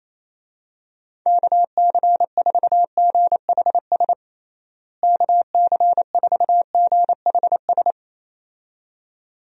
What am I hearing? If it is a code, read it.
Morse code, heard as a keyed tone, about 28 wpm, 700 Hz. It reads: KC4GHS KC4GHS
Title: USA Call Signs - 6 Characters - Repeated 2x - 28wpm